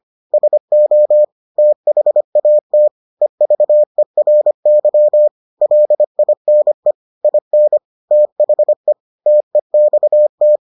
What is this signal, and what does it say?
Morse code, heard as a keyed tone, about 25 wpm, 600 Hz. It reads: SO THAT EVERY LINE IN THE TEXT